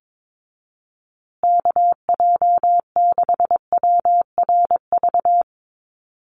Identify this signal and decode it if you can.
Morse code, heard as a keyed tone, about 22 wpm, 700 Hz. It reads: KJ6WRV
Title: USA Call Signs Repeated 2x 22wpm